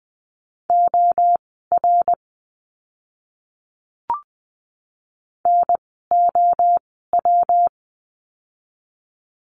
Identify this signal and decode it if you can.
Morse code, heard as a keyed tone, about 20 wpm, spaced as for 15 wpm, 700 Hz. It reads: OR NOW
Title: 139 Lesson 23f - Letter f - Words 20wpm